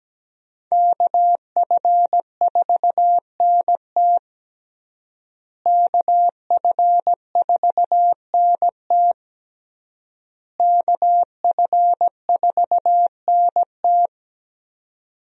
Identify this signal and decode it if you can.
Morse code, heard as a keyed tone, about 17 wpm, 700 Hz. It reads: KF4NT KF4NT KF4NT
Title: USA Call Signs - 5 Characters - Repeated 3x - 17wpm